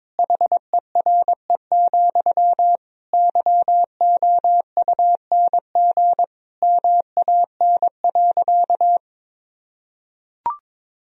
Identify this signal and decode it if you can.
Morse code, heard as a keyed tone, about 22 wpm, 700 Hz. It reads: HERE, YOUNG MAN.
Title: Sentences from Top 1000 Words - Review Part 1 - 22wpm